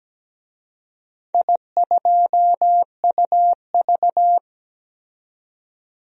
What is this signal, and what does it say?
Morse code, heard as a keyed tone, about 17 wpm, 700 Hz. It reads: I2UV
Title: DXCC-100 Call Signs - 4 Characters - 17wpm